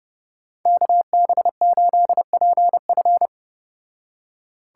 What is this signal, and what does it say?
Morse code, heard as a keyed tone, about 30 wpm, 700 Hz. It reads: KB8PF